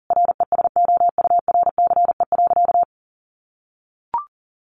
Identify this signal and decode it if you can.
Morse code, heard as a keyed tone, about 40 wpm, 700 Hz. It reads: RESOURCE.
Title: Sentences from Top 1300 Words 40wpm